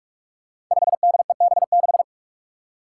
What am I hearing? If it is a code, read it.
Morse code, heard as a keyed tone, about 44 wpm, 700 Hz. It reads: 5DEB6
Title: Five Challenging Letter-Number Combinations 45wpm